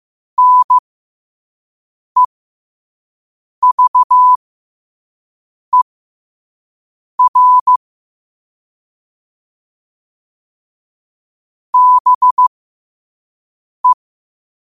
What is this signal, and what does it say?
Morse code, heard as a keyed tone, about 15 wpm, spaced as for 5 wpm, 1000 Hz. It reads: NEVER BE
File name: 5wpm